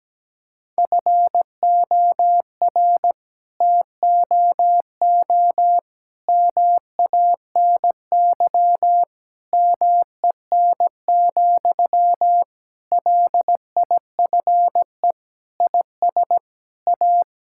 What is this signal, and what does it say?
Morse code, heard as a keyed tone, about 17 wpm, 700 Hz. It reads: FOR TOO MANY MEN, LIFE IS A
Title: Sentences from Top 1000 Words - Review Part 6 - 17wpm